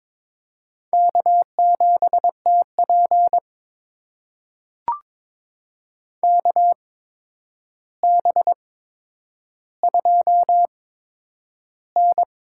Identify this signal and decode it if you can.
Morse code, heard as a keyed tone, about 22 wpm, spaced as for 9 wpm, 700 Hz. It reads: K7TP KB2N
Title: USA Call Signs - 4 Characters - 8x Character Spacing - 22wpm